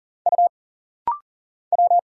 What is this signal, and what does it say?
Morse code, heard as a keyed tone, about 40 wpm, 700 Hz. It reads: U W
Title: Single Letter-Number Alt - Rapid-Fire 40wpm